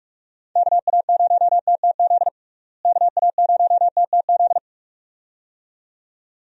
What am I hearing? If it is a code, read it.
Morse code, heard as a keyed tone, about 46 wpm, 700 Hz. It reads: KA0TTZ KA0TTZ
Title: USA Call Signs - 6 Characters - Repeated 2x - 45wpm